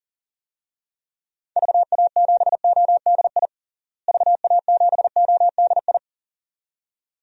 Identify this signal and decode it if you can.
Morse code, heard as a keyed tone, about 40 wpm, 700 Hz. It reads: VA7ODI VA7ODI
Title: DXCC-100 Call Signs - 6 Characters - Repeated 2x - 40wpm